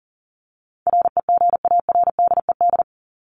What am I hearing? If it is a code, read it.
Morse code, heard as a keyed tone, about 40 wpm, 700 Hz. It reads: REGARDED
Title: ED Suffix - Top 100 Words with Base Words 40wpm